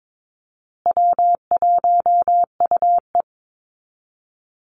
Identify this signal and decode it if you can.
Morse code, heard as a keyed tone, about 22 wpm, 700 Hz. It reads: W1UE
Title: Easy Call Signs Repeated 3x - 22wpm